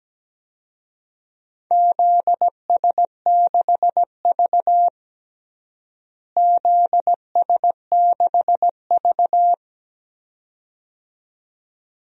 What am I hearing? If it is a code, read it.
Morse code, heard as a keyed tone, about 17 wpm, 700 Hz. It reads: ZS6V ZS6V
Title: DXCC-100 Call Signs - 4 Characters - Repeated 2x - 17wpm